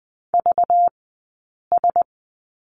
Morse code, heard as a keyed tone, about 20 wpm, spaced as for 8 wpm, 700 Hz. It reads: VS
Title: 190 Lesson 31d - Letter v - Two Characters 20wpm